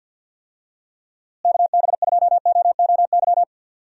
Morse code, heard as a keyed tone, about 50 wpm, 700 Hz. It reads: KB2OOY
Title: USA Call Signs - 6 Characters - Repeated 2x - 50wpm